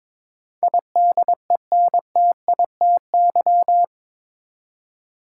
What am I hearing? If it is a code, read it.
Morse code, heard as a keyed tone, about 22 wpm, 700 Hz. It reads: IDENTITY
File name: practice / 22wpm / Eight Letter Words - 22wpm